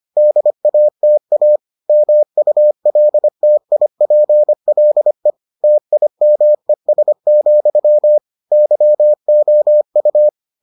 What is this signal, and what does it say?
Morse code, heard as a keyed tone, about 25 wpm, 600 Hz. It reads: DATA MULTIPLE TIMES, YOU